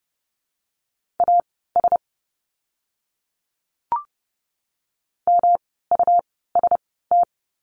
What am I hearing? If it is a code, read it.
Morse code, heard as a keyed tone, about 30 wpm, spaced as for 17 wpm, 700 Hz. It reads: AS MUST
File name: course / 30:10wpm / 237 Lesson 38f - Letter q - Words 30wpm